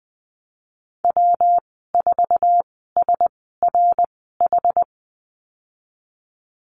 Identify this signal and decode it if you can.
Morse code, heard as a keyed tone, about 20 wpm, spaced as for 15 wpm, 700 Hz. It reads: W4SRH
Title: 167 Lesson 27g - Number 7 - Call Signs 20wpm